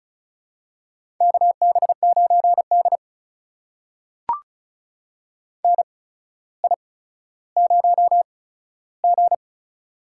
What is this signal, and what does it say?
Morse code, heard as a keyed tone, about 36 wpm, spaced as for 15 wpm, 700 Hz. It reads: KB9D NI0G
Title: USA Call Signs - 4 Characters - 8x Character Spacing - 35wpm